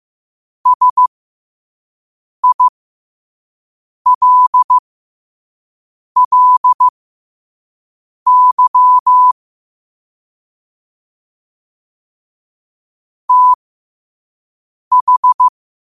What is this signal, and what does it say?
Morse code, heard as a keyed tone, about 15 wpm, spaced as for 5 wpm, 1000 Hz. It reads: SILLY TH
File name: qotdfiles / 5wpm